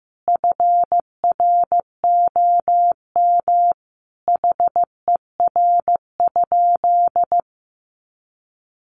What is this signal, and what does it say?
Morse code, heard as a keyed tone, about 15 wpm, 700 Hz. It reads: FROM HER?
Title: Sentences from Top 900 Words 15wpm